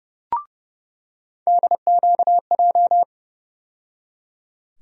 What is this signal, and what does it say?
Morse code, heard as a keyed tone, about 32 wpm, 700 Hz. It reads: DQJ
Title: USA Call Sign Postfixes - 3 Letters 30wpm